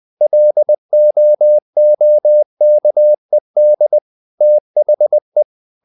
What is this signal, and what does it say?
Morse code, heard as a keyed tone, about 20 wpm, 600 Hz. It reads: LOOKED THE
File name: the old man and the sea part 2 20wpm